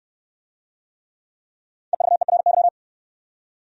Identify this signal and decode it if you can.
Morse code, heard as a keyed tone, about 65 wpm, 700 Hz. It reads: EVERY